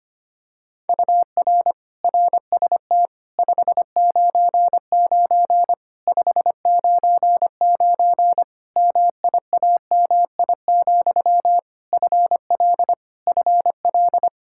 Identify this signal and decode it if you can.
Morse code, heard as a keyed tone, about 25 wpm, 700 Hz. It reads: UR RST 599 599 MIAMI, FL FL
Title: Practice QSO - Exchange 1 - Beginning - 25wpm